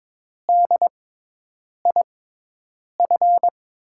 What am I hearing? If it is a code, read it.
Morse code, heard as a keyed tone, about 22 wpm, spaced as for 8 wpm, 700 Hz. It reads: DIF
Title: USA Call Signs - 6x Character Spacing - 22wpm